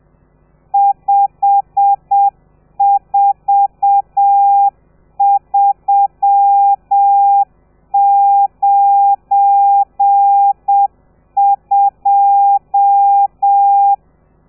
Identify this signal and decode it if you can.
Morse code, heard as a keyed tone, about 7 wpm, 800 Hz. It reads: 54392
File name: telegrafia / i2tfj / Lez12 30